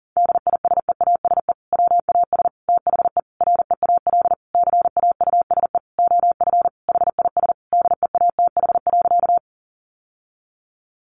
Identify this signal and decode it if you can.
Morse code, heard as a keyed tone, about 40 wpm, 700 Hz. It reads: DISEASE WAS THE REAL CAUSE OF HIS DEATH.
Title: Sentences from Top 1100 Words 40wpm